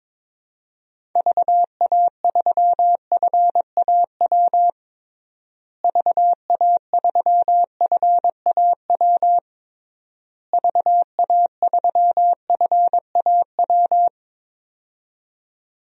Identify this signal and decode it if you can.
Morse code, heard as a keyed tone, about 22 wpm, 700 Hz. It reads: VA3FAW VA3FAW VA3FAW
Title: DXCC-100 Call Signs - 6 Characters - Repeated 3x - 22wpm